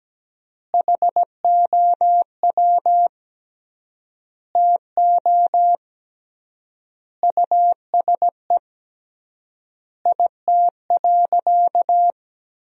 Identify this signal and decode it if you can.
Morse code, heard as a keyed tone, about 17 wpm, 700 Hz. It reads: HOW TO USE IT.
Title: 3x Word-Spacing - Sentences from Top 500 Words - 17wpm